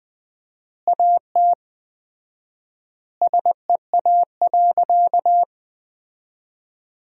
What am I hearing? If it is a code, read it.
Morse code, heard as a keyed tone, about 20 wpm, 700 Hz. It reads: AT SEA.
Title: 4x Word-Spacing - Sentences from Top 300 Words - 20wpm